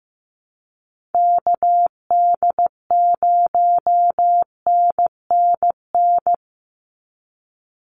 Morse code, heard as a keyed tone, about 15 wpm, 700 Hz. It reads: KD0NNN
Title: USA Call Signs 15wpm